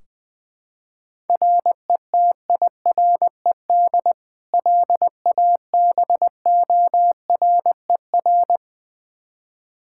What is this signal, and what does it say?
Morse code, heard as a keyed tone, about 21 wpm, 700 Hz. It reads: RETIRED LABORER
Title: Practice QSO - Exchange 3 - More Info - 20wpm